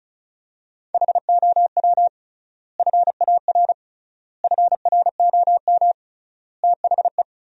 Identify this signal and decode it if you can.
Morse code, heard as a keyed tone, about 36 wpm, 700 Hz. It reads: HOW FAR FROM THE
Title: 3x Word-Spacing - Sentences from Top 400 Words - 35wpm